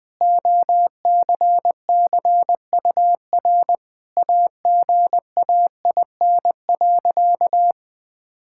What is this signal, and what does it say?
Morse code, heard as a keyed tone, about 21 wpm, 700 Hz. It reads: OCCUR AGAIN.
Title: Sentences from Top 1000 Words - Review Part 6 - 20wpm